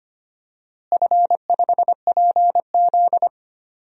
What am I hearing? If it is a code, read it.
Morse code, heard as a keyed tone, about 25 wpm, 700 Hz. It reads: F5PZ
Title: DXCC-100 Call Signs - 4 Characters - Repeated 3x - 25wpm